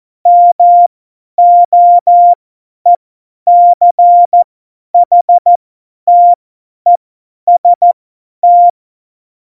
Morse code, heard as a keyed tone, about 14 wpm, spaced as for 10 wpm, 700 Hz. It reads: MOECHTEST
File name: Morsekurs / Witze / Witz 22. Januar 2019, 50 F70